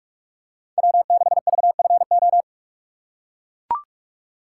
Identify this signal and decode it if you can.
Morse code, heard as a keyed tone, about 43 wpm, 700 Hz. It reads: W6VFO
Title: USA Call Signs - 6x Character Spacing - 45wpm